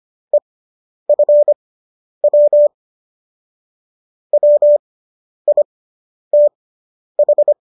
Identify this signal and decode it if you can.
Morse code, heard as a keyed tone, about 25 wpm, spaced as for 10 wpm, 590 Hz. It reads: EFW WITH